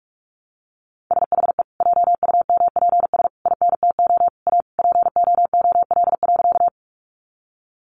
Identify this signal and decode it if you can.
Morse code, heard as a keyed tone, about 44 wpm, 700 Hz. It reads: SHE JUMPS INTO A POOL.
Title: Additional Easy Phrases 45wpm